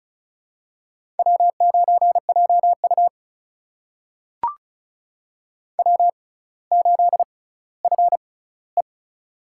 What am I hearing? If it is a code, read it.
Morse code, heard as a keyed tone, about 34 wpm, spaced as for 18 wpm, 700 Hz. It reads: W9JU W8FE
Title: USA Call Signs - 6x Character Spacing - 35wpm